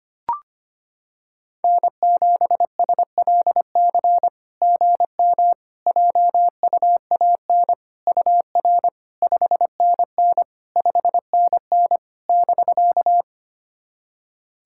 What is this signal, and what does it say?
Morse code, heard as a keyed tone, about 25 wpm, 690 Hz. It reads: N7SLC GM JUAN UR 5NN 5NN <BK>
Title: SOTA QSO Practice 25wpm